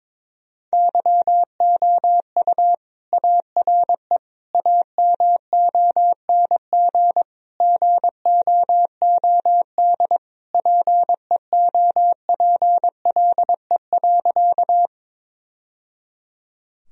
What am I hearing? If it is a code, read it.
Morse code, heard as a keyed tone, about 22 wpm, 700 Hz. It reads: YOU ARE AMONG GOOD PEOPLE.
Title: Sentences from Top 1000 Words - Review Part 3 - 22wpm